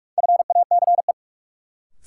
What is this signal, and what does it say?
Morse code, heard as a keyed tone, about 43 wpm, 700 Hz. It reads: FACE
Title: Sentences from Top 300 Words - New Word Drill - 45wpm